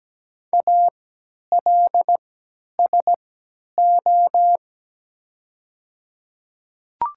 Morse code, heard as a keyed tone, about 17 wpm, spaced as for 10 wpm, 700 Hz. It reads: ALSO
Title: 3x Character Spacing - Top 500 Words - 17wpm